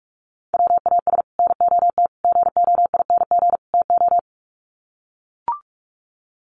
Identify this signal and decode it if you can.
Morse code, heard as a keyed tone, about 44 wpm, 700 Hz. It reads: WAS NOT GOING TO
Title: Sets of 4 Words 45wpm